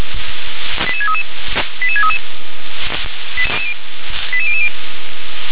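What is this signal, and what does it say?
Five-tone selective call (ZVEI), heard as quick runs of five tones, tones of about 70 ms. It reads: 88633 88633 88000
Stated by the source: ZVEI five-tone call 88720 received via SDR.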